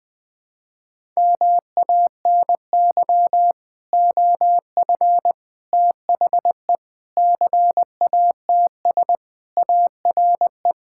Morse code, heard as a keyed tone, about 20 wpm, 700 Hz. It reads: MANY OF THE CATS ARE
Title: Sentences with 3-Word Phrases - 20wpm